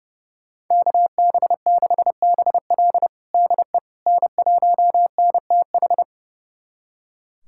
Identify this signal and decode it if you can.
Morse code, heard as a keyed tone, about 32 wpm, 700 Hz. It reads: KB6BL DE N1NTH
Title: USA Callsign DE Callsign - 5 Characters - 30wpm